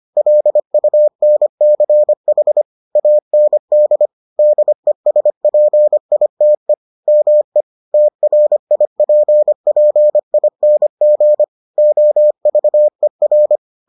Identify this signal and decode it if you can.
Morse code, heard as a keyed tone, about 25 wpm, 600 Hz. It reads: LUNCH AND DESPITE ME TRIPPING OVER